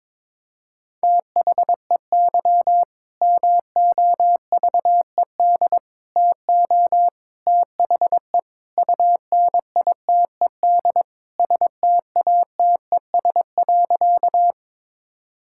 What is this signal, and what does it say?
Morse code, heard as a keyed tone, about 22 wpm, 700 Hz. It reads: THEY MOVED TO THE UNITED STATES.
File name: practice / 22wpm / Sentences with 4-Word Phrases - 22wpm